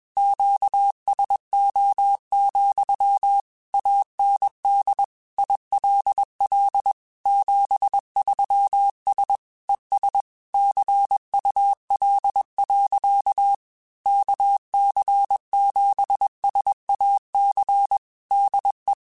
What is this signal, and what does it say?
Morse code, heard as a keyed tone, about 21 wpm, 800 Hz. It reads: QSO, AND ILL 73S ES CUL. KC7SAC DE
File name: QSOs / QSOs-mp3s / kc7sac@20